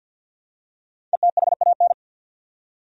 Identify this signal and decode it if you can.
Morse code, heard as a keyed tone, about 51 wpm, 700 Hz. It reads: ETHAN
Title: USA Common Names - 50wpm